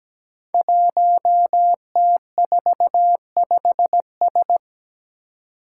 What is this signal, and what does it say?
Morse code, heard as a keyed tone, about 17 wpm, 700 Hz. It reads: 1T45S
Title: Five Challenging Letter-Number Combinations 17wpm